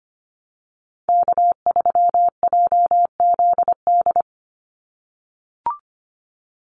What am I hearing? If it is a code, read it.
Morse code, heard as a keyed tone, about 25 wpm, 700 Hz. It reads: K3JZD
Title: NA SOTA Chaser Callsigns 25wpm